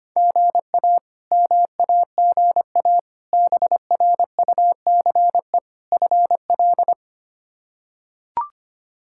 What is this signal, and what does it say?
Morse code, heard as a keyed tone, about 25 wpm, 700 Hz. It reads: GA MAGA BRUCE FL
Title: K1USN SST QSO Practice - 25wpm